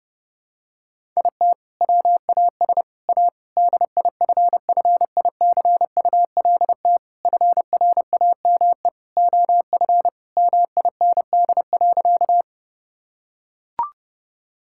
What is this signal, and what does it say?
Morse code, heard as a keyed tone, about 30 wpm, 700 Hz. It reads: IT WAS A DIFFICULT FRAME OF MIND.
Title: Sentences from Top 1000 Words - Review Part 6 - 30wpm